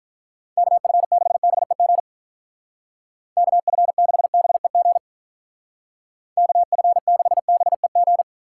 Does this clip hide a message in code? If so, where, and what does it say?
Yes: Morse code from 0.6 to 8.2 s, heard as a keyed tone, about 45 wpm, 700 Hz. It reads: KF6BEG KF6BEG KF6BEG